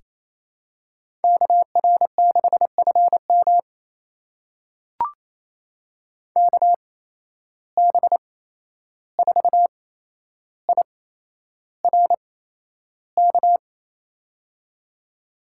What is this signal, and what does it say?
Morse code, heard as a keyed tone, about 28 wpm, spaced as for 11 wpm, 700 Hz. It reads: KR6FM KB4IRK
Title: USA Call Signs - 8x Character Spacing - 28wpm